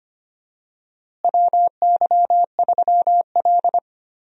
Morse code, heard as a keyed tone, about 25 wpm, 700 Hz. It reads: WY3L
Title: USA Call Signs - 4 Characters - Repeated 3x - 25wpm